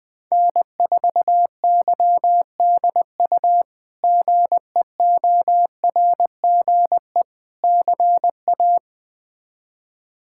Morse code, heard as a keyed tone, about 21 wpm, 700 Hz. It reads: N4YDU GEORGE CA
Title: K1USN SST QSO Practice - 20wpm